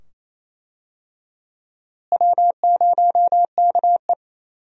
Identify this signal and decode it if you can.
Morse code, heard as a keyed tone, about 28 wpm, 700 Hz. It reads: W0KE
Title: USA Call Signs - 4 Characters - Repeated 2x - 28wpm